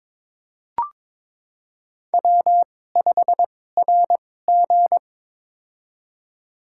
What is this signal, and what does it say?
Morse code, heard as a keyed tone, about 22 wpm, spaced as for 16 wpm, 700 Hz. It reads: W5RG
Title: USA Call Signs - 4 Characters - 2x Character Spacing - 22wpm